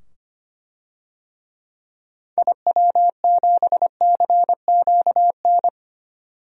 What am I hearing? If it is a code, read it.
Morse code, heard as a keyed tone, about 25 wpm, 700 Hz. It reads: IW7CQN
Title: DXCC-100 Call Signs - 6 Characters - 25wpm